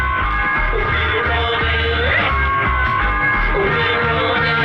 Emergency Alert System (SAME) tones, heard as harsh buzzing bursts. It is heard from 0.9 to 3.5 s.